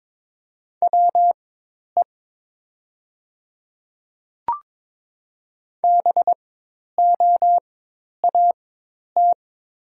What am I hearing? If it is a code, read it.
Morse code, heard as a keyed tone, about 22 wpm, spaced as for 10 wpm, 700 Hz. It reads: WE BOAT
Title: 4x Character Spacing - Top 500 Words - 22wpm